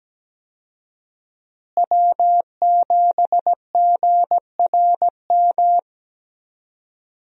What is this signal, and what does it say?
Morse code, heard as a keyed tone, about 17 wpm, 700 Hz. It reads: W7GRM
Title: USA Call Signs - 5 Characters - Repeated 2x - 17wpm